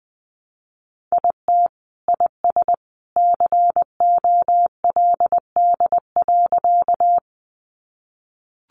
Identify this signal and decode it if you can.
Morse code, heard as a keyed tone, about 20 wpm, 700 Hz. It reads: IT IS COLD.
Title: Sentences from Top 400 Words 20wpm